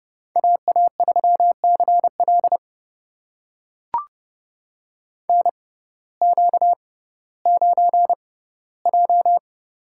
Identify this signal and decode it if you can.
Morse code, heard as a keyed tone, about 31 wpm, spaced as for 17 wpm, 700 Hz. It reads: AA3CL NQ9J
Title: USA Call Signs - 6x Character Spacing - 30wpm